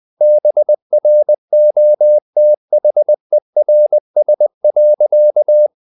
Morse code, heard as a keyed tone, about 20 wpm, 600 Hz. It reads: BROTHERS.
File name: the old man and the sea part 2 20wpm